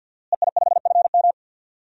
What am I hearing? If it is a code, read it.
Morse code, heard as a keyed tone, about 47 wpm, 700 Hz. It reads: EI5FM
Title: DXCC-100 Call Signs - 5 Characters - Repeated 2x - 50wpm